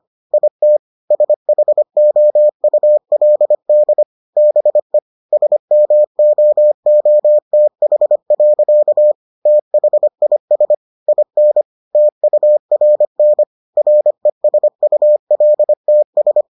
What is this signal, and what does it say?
Morse code, heard as a keyed tone, about 25 wpm, 600 Hz. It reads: IT SHOULD BE SMOOTH. THIS IN TURN RESULTS